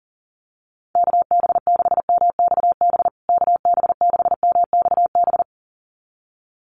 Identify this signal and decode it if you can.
Morse code, heard as a keyed tone, about 40 wpm, 700 Hz. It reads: KB6MXB KB6MXB
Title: USA Call Signs Repeated 2x 40wpm